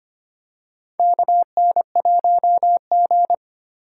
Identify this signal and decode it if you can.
Morse code, heard as a keyed tone, about 25 wpm, 700 Hz. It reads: KN1G
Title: USA Call Signs - 3x Character Spacing - 25wpm